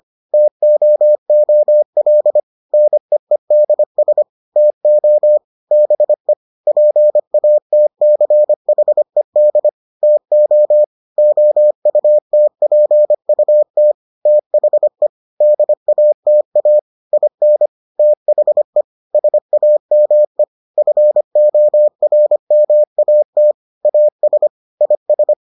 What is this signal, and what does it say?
Morse code, heard as a keyed tone, about 25 wpm, 600 Hz. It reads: TOOL NEEDS TO BE PATCHED TO OUTPUT THE DATA IN THE SAME FORMAT AS IS